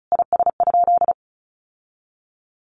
Morse code, heard as a keyed tone, about 34 wpm, 700 Hz. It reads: IS?
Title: Sentences from Top 1400 Words 35wpm